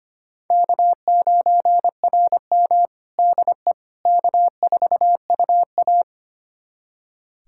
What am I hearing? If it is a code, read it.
Morse code, heard as a keyed tone, about 25 wpm, 700 Hz. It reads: K9RM DE K4UA
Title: USA Callsign DE Callsign - 4 Characters - 25wpm